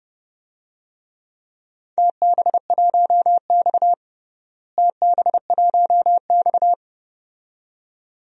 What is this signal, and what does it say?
Morse code, heard as a keyed tone, about 30 wpm, 700 Hz. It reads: TB1X TB1X
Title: DXCC-100 Call Signs - 4 Characters - Repeated 2x - 30wpm